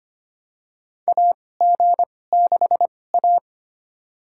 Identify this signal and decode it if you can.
Morse code, heard as a keyed tone, about 25 wpm, spaced as for 18 wpm, 700 Hz. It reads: AG6A
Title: USA Call Signs - 4 Characters - 2x Character Spacing - 25wpm